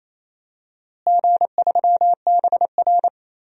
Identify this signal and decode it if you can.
Morse code, heard as a keyed tone, about 28 wpm, 700 Hz. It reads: G3BR
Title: DXCC-100 Call Signs - 4 Characters - Repeated 3x - 28wpm